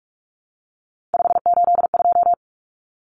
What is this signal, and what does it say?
Morse code, heard as a keyed tone, about 44 wpm, 700 Hz. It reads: 582
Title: Numbers - 3-Digit 45wpm